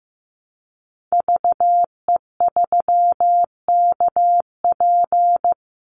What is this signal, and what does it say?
Morse code, heard as a keyed tone, about 15 wpm, 700 Hz. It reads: VE3KP
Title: Call Signs Repeated 3x - 15wpm